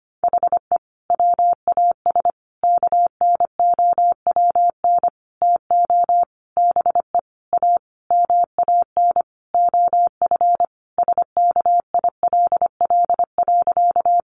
Morse code, heard as a keyed tone, about 25 wpm, 700 Hz. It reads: HE WAS KNOWN TO BE A MAN OF SKILL.
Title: Sentences from Top 1200 Words 25wpm